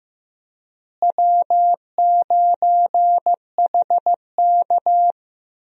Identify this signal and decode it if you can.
Morse code, heard as a keyed tone, about 15 wpm, 700 Hz. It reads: W9HK
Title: USA Call Signs - 4 Characters - Repeated 2x - 15wpm